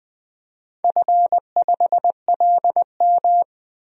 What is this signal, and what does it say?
Morse code, heard as a keyed tone, about 21 wpm, 700 Hz. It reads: F5LM
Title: DXCC-100 Call Signs - 4 Characters - 20wpm